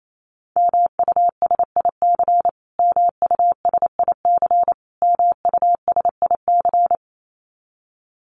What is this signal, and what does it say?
Morse code, heard as a keyed tone, about 28 wpm, 700 Hz. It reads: MUSIC MUSIC MUSIC
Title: Top 500 Words Repeated 28wpm